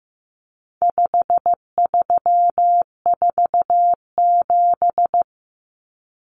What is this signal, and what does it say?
Morse code, heard as a keyed tone, about 15 wpm, 700 Hz. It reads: 5347
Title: Common Cut Numbers - 4-Digit 15wpm